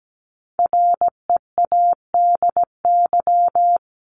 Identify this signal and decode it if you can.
Morse code, heard as a keyed tone, about 17 wpm, 700 Hz. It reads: READY
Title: Top 500 Words 17wpm